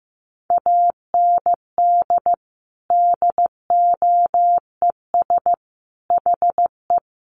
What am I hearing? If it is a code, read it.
Morse code, heard as a keyed tone, about 15 wpm, 700 Hz. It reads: AND DOES HE
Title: Sentences from Top 1500 Words 15wpm